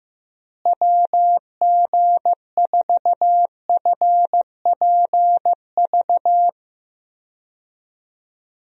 Morse code, heard as a keyed tone, about 15 wpm, 700 Hz. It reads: WG4FPV
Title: USA Call Signs - 6 Characters - Repeated 2x - 15wpm